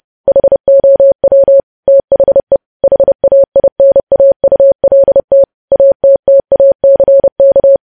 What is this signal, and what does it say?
Morse code, heard as a keyed tone, about 30 wpm, 550 Hz. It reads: HOW THE HAINAULT ATTACK
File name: podcast / News headlines-1x-30x30